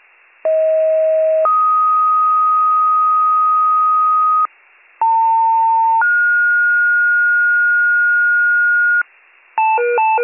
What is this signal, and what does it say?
Two-tone page, heard as a one-second tone, then a three-second tone, about 640 then 1250 Hz, about 900 then 1430 Hz, with noise. It is heard from 0.4 to 9.0 s.